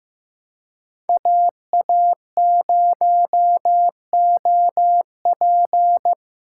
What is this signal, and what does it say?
Morse code, heard as a keyed tone, about 15 wpm, 700 Hz. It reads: AA0OP
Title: USA Callsign DE Callsign - 5 Characters - 15wpm